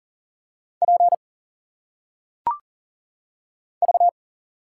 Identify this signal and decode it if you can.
Morse code, heard as a keyed tone, about 40 wpm, 700 Hz. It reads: P V
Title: Single Letters - Phonetic Pronunciation 40wpm